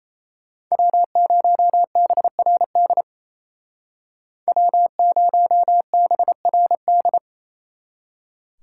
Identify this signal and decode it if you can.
Morse code, heard as a keyed tone, about 36 wpm, 700 Hz. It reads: W0BRD W0BRD